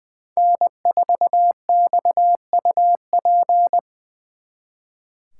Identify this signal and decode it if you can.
Morse code, heard as a keyed tone, about 23 wpm, 700 Hz. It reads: N4XUP
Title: Speed Racing - USA 1x3 Call Signs 20wpm